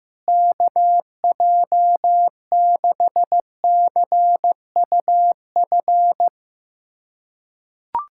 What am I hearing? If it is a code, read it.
Morse code, heard as a keyed tone, about 15 wpm, 700 Hz. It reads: KJ6CUF
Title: USA Callsign DE Callsign - 6 Characters - 15wpm